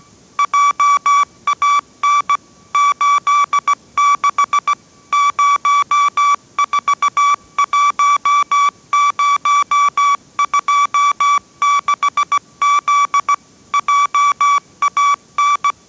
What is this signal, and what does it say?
Morse code, heard as a keyed tone, about 17 wpm, 1150 Hz. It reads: JAN 86 041026Z JAN
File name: 18wpm